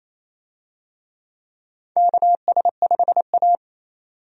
Morse code, heard as a keyed tone, about 28 wpm, 700 Hz. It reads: KS5A
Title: USA Call Signs - 4 Characters - 28wpm